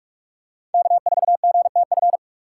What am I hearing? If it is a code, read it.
Morse code, heard as a keyed tone, about 42 wpm, 700 Hz. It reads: K4GTF
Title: USA Call Signs - 5 Characters - 8x Character Spacing - 45wpm